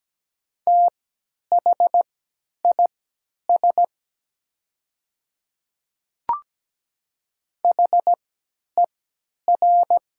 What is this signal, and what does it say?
Morse code, heard as a keyed tone, about 17 wpm, spaced as for 10 wpm, 700 Hz. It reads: THIS HER
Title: 3x Character Spacing - Top 100 Words - 17wpm